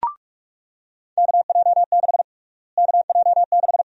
Morse code, heard as a keyed tone, about 46 wpm, 700 Hz. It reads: KJ6 KJ6
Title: USA Call Sign Prefixes - Repeated 2x - 45wpm